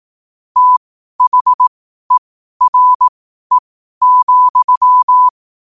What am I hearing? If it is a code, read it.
Morse code, heard as a keyed tone, about 18 wpm, spaced as for 12 wpm, 1000 Hz. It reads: THERE,
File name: qotdfiles / 12wpm